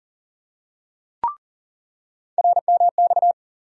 Morse code, heard as a keyed tone, about 37 wpm, 700 Hz. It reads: RMX
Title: USA Call Sign Postfixes - 3 Letters 40wpm